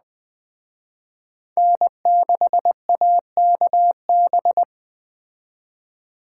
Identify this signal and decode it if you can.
Morse code, heard as a keyed tone, about 19 wpm, 700 Hz. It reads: N6AKB